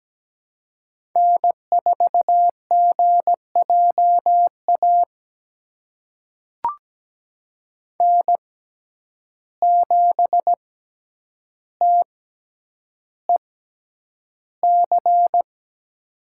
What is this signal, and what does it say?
Morse code, heard as a keyed tone, about 17 wpm, spaced as for 9 wpm, 700 Hz. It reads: N4GJA N7TEC
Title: USA Call Signs - 5 Characters - 6x Character Spacing - 17wpm